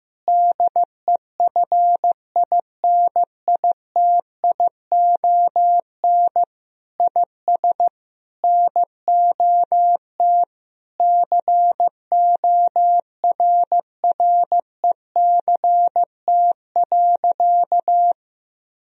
Morse code, heard as a keyed tone, about 15 wpm, 700 Hz. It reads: DEFINITION IS NOT CORRECT.
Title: Sentences from Top 1000 Words - Review Part 4 - 15wpm